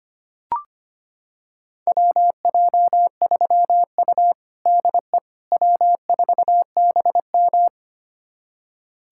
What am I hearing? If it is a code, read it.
Morse code, heard as a keyed tone, about 25 wpm, 700 Hz. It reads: WJ3U DE W4BM
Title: USA Callsign DE Callsign - 4 Characters - 25wpm